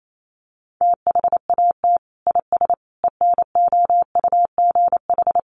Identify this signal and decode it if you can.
Morse code, heard as a keyed tone, about 28 wpm, 700 Hz. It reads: THAT IS ENOUGH
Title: Sentences from Top 400 Words 28wpm